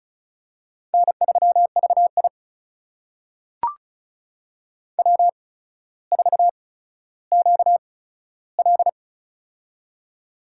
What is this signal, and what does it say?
Morse code, heard as a keyed tone, about 36 wpm, spaced as for 15 wpm, 700 Hz. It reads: N3VI W4QL
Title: USA Call Signs - 4 Characters - 8x Character Spacing - 35wpm